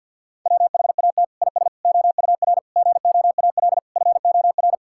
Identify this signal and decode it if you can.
Morse code, heard as a keyed tone, about 48 wpm, 700 Hz. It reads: WHAT IS OUR GOAL FOR
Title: Sentences from Top 1000 Words - Review Part 5 - 50wpm